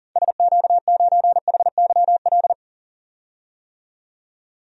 Morse code, heard as a keyed tone, about 40 wpm, 700 Hz. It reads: SQ9HYL
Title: DXCC-100 Call Signs - 6 Characters - 40wpm